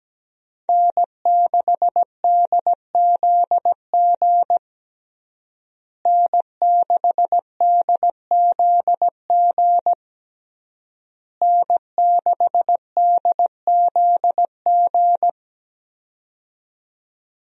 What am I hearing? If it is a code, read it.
Morse code, heard as a keyed tone, about 17 wpm, 700 Hz. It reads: N6DZG N6DZG N6DZG
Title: USA Call Signs - 5 Characters - Repeated 3x - 17wpm